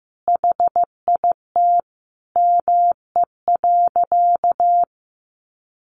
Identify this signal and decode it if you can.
Morse code, heard as a keyed tone, about 15 wpm, 700 Hz. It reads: HIT ME.
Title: Sentences from Top 1200 Words 15wpm